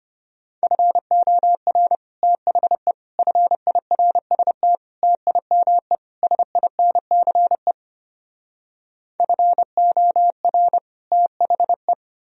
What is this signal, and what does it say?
Morse code, heard as a keyed tone, about 33 wpm, 700 Hz. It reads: FOR THE FIRST TIME SINCE FOR THE